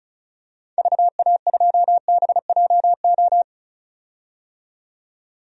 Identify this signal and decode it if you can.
Morse code, heard as a keyed tone, about 34 wpm, 700 Hz. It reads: VA2BJO
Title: DXCC-100 Call Signs - 6 Characters - 35wpm